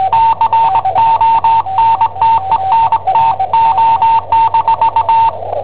Morse code, heard as a keyed tone, about 20 wpm, 910 Hz. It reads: CONTENTO-
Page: Cassetta esercizi preparata da I7OHP per IW5ELP